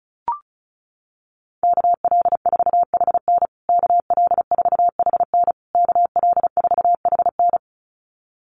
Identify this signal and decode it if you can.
Morse code, heard as a keyed tone, about 34 wpm, 700 Hz. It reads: KL4HN KL4HN KL4HN
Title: USA Call Signs Repeated 3x 35wpm